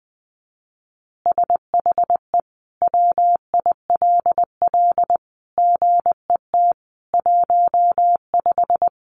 Morse code, heard as a keyed tone, about 20 wpm, 700 Hz. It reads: SHE WILL GET 15
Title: Easy Phrases 20wpm